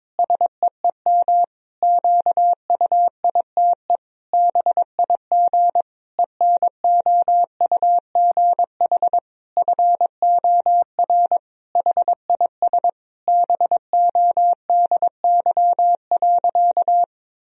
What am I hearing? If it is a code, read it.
Morse code, heard as a keyed tone, about 22 wpm, 700 Hz. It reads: SEEM QUITE BIG ENOUGH FOR HIS BODY.
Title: Sentences from Top 1000 Words - Review Part 2 - 22wpm